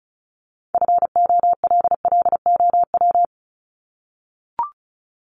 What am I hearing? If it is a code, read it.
Morse code, heard as a keyed tone, about 34 wpm, 700 Hz. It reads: FOLLOW
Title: ING Suffix - Top 100 Words with Base Words 35wpm